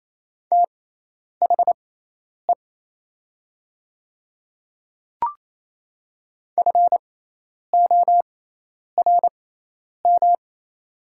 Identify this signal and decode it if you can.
Morse code, heard as a keyed tone, about 28 wpm, spaced as for 10 wpm, 700 Hz. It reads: THE FORM
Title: 6x Character Spacing - Top 200 Words - 28wpm